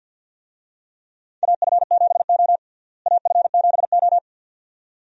Morse code, heard as a keyed tone, about 50 wpm, 700 Hz. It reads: AF7O AF7O
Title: USA Call Signs - 4 Characters - Repeated 2x - 50wpm